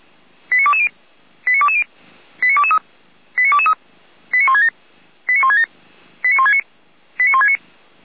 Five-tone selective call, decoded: 89220 89220 89223 89223 89167 89167 89169 89169

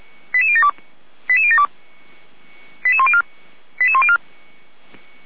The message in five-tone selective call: 88072 88072 88104 88104

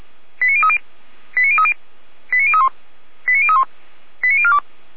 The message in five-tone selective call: 89030 89030 89031 89031 89042

